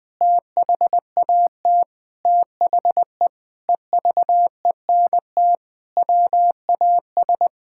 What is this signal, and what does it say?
Morse code, heard as a keyed tone, about 20 wpm, 700 Hz. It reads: THAT THE EVENT WAS
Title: Sentences with 7-Word Phrases - 20wpm